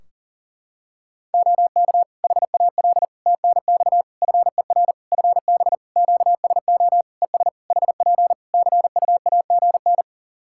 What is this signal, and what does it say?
Morse code, heard as a keyed tone, about 40 wpm, 700 Hz. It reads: OK HAL TNX FER FB QSO ES HP CUAGN
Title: Practice QSO - Exchange 4 - Ending - 40wpm